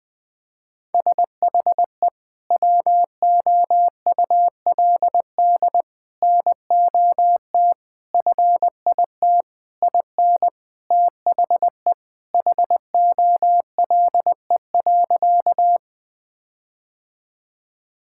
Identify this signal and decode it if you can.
Morse code, heard as a keyed tone, about 20 wpm, 700 Hz. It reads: SHE WOULD NOT FIT IN THE HOLE.
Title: Sentences from Top 1600 Words 20wpm